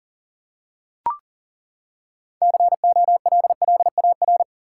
Morse code, heard as a keyed tone, about 40 wpm, 700 Hz. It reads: COLLAR
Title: Six Letter Words - 40wpm